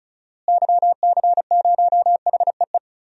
Morse code, heard as a keyed tone, about 34 wpm, 700 Hz. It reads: YC0HEE
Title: DXCC-100 Call Signs - 6 Characters - Repeated 2x - 35wpm